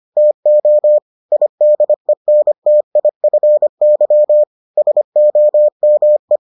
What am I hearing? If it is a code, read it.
Morse code, heard as a keyed tone, about 25 wpm, 600 Hz. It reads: TO IDENTIFY SOME